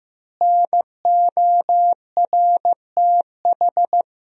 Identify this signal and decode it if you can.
Morse code, heard as a keyed tone, about 15 wpm, 700 Hz. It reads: NORTH
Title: 3x Word-Spacing - Sentences from Top 400 Words - 15wpm